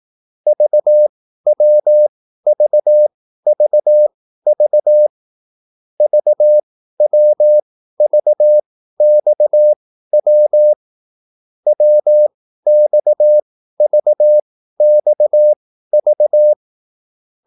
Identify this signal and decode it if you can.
Morse code, heard as a keyed tone, about 18 wpm, spaced as for 13 wpm, 600 Hz. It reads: VWVVV VWVXW WXVXV
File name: QSL / CW lesson